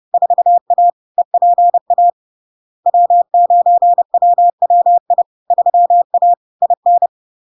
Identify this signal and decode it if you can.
Morse code, heard as a keyed tone, about 30 wpm, 700 Hz. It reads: 4A EPA W9WWI 3A IN
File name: LARK-FD-2022-Log-in-CW-30wpm-700-Hz